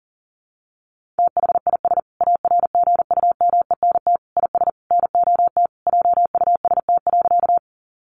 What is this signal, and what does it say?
Morse code, heard as a keyed tone, about 40 wpm, 700 Hz. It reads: THIS ARGUMENT IS NOT JUST.
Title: Sentences from Top 1400 Words 40wpm